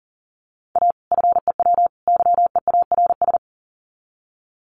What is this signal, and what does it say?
Morse code, heard as a keyed tone, about 40 wpm, 700 Hz. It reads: A FEW YEARS
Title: Sets of 3 Words 40wpm